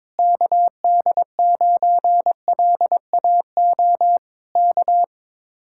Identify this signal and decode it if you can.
Morse code, heard as a keyed tone, about 22 wpm, 700 Hz. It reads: KD9LAO K
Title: Speed-Racing Common QSO Phrases Encore - 22wpm